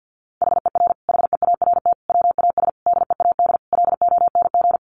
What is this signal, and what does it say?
Morse code, heard as a keyed tone, about 50 wpm, 700 Hz. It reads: HER HEART WAS DEAD LONG
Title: Sentences from Top 1500 Words 50wpm